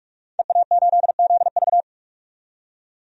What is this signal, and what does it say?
Morse code, heard as a keyed tone, about 46 wpm, 700 Hz. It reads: EA8ZV
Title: DXCC-100 Call Signs - 5 Characters - 45wpm